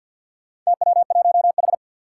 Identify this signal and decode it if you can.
Morse code, heard as a keyed tone, about 46 wpm, 700 Hz. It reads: TW1H
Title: Four Challenging Letter-Number Combinations 50wpm